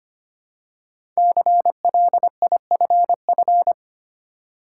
Morse code, heard as a keyed tone, about 25 wpm, 700 Hz. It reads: CLIFF
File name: 5 Letter Words - First 3 Letters Given - 25wpm